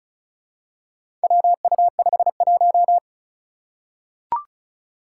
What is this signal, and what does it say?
Morse code, heard as a keyed tone, about 36 wpm, 700 Hz. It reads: WU51
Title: Four Challenging Letter-Number Combinations 35wpm